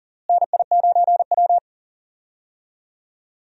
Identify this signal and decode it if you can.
Morse code, heard as a keyed tone, about 40 wpm, 700 Hz. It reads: NI9W
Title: USA Call Signs - 4 Characters - Repeated 3x - 40wpm